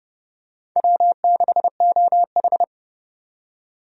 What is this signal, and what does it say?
Morse code, heard as a keyed tone, about 29 wpm, 700 Hz. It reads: W6OH
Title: USA Call Signs - 4 Characters - 30wpm